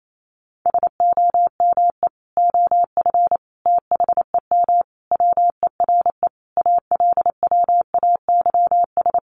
Morse code, heard as a keyed tone, about 28 wpm, 700 Hz. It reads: SOME OF THEM WERE ALWAYS
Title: Sentences from Top 1200 Words 28wpm